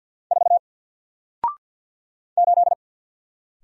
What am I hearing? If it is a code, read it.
Morse code, heard as a keyed tone, about 48 wpm, 700 Hz. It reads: 4 8
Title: Single Character (Letter:Number:Punctuation:Prosign) 50wpm